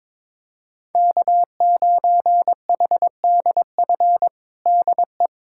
Morse code, heard as a keyed tone, about 22 wpm, 700 Hz. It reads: K9HDF DE
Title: USA Callsign DE Callsign - 5 Characters - 22wpm